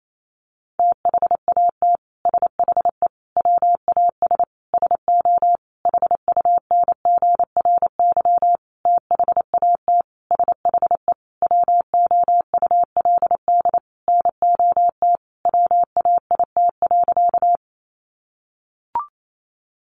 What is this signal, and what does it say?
Morse code, heard as a keyed tone, about 28 wpm, 700 Hz. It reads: THAT SHE WAS SO HUNGRY THAT SHE WOULD NOT WAIT.
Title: Sentences from Top 1400 Words 28wpm